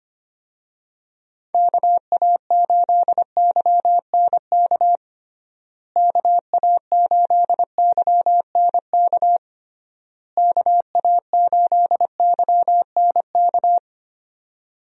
Morse code, heard as a keyed tone, about 25 wpm, 700 Hz. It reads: KA8YNK KA8YNK KA8YNK
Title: USA Call Signs - 6 Characters - Repeated 3x - 25wpm